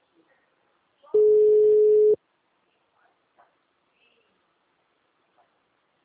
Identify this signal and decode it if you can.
Ringback tone (Europe, 425 Hz).
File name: out- -2010